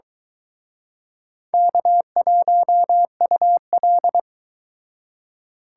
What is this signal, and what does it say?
Morse code, heard as a keyed tone, about 23 wpm, 700 Hz. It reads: K1UL